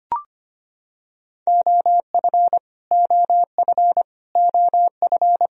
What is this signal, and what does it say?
Morse code, heard as a keyed tone, about 25 wpm, 700 Hz. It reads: OF OF OF
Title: Top 100 Words Repeated 25wpm